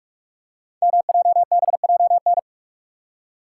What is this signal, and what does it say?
Morse code, heard as a keyed tone, about 44 wpm, 690 Hz. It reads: MJBJN
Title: Code Groups - 5 Letters - 45wpm